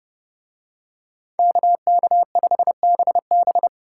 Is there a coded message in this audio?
Yes: Morse code from 1.4 to 3.7 s, heard as a keyed tone, about 29 wpm, 700 Hz. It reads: KK5BB